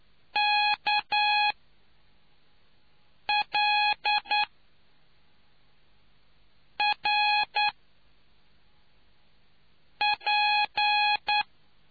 Morse code, heard as a keyed tone, about 9 wpm, 830 Hz. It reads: K L R P